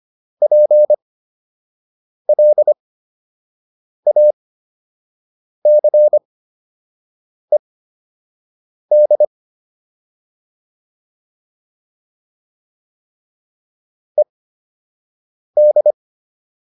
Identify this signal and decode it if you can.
Morse code, heard as a keyed tone, about 25 wpm, spaced as for 6 wpm, 600 Hz. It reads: PLACED ED